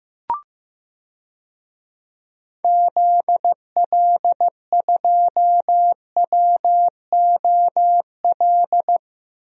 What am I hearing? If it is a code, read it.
Morse code, heard as a keyed tone, about 16 wpm, 700 Hz. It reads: ZL2WOL
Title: DXCC-100 Call Signs - 6 Characters - 15wpm